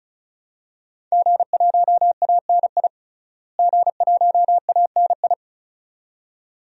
Morse code, heard as a keyed tone, about 34 wpm, 700 Hz. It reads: G1ANI G1ANI
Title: DXCC-100 Call Signs - 5 Characters - Repeated 2x - 35wpm